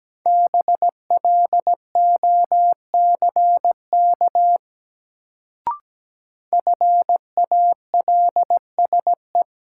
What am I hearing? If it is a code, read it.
Morse code, heard as a keyed tone, about 17 wpm, 700 Hz. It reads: BLOCK FALSE
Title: Sentences from Top 1600 Words - New Word Drill Alt - 17wpm